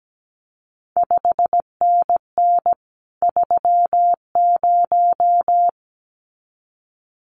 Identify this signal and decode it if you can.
Morse code, heard as a keyed tone, about 17 wpm, 700 Hz. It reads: 5NN 30
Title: CQ WW DX Contest Exchange Encore - 17wpm